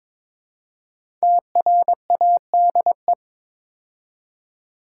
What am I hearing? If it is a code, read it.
Morse code, heard as a keyed tone, about 22 wpm, 700 Hz. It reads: TRADE
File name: 5 Letter Words - First 3 Letters Given - 22wpm